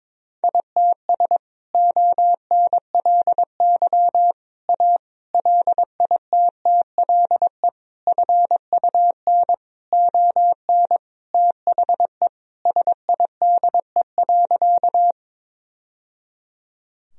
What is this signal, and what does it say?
Morse code, heard as a keyed tone, about 22 wpm, 700 Hz. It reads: ITS ONLY A LITTLE FUN ON THE SIDE.
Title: Sentences from Top 1000 Words - Review Part 4 - 22wpm